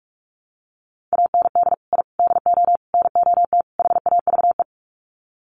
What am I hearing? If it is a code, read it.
Morse code, heard as a keyed tone, about 44 wpm, 700 Hz. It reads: AND I DO NOT HAVE
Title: Sets of 5 Words Encore 45wpm